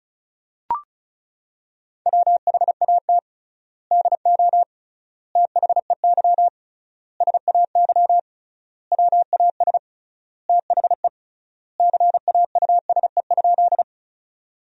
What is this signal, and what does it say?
Morse code, heard as a keyed tone, about 36 wpm, 700 Hz. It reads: WHAT DO THEY SAY WAS THE CAUSE?
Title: 3x Word-Spacing - Sentences from Top 200 Words - 35wpm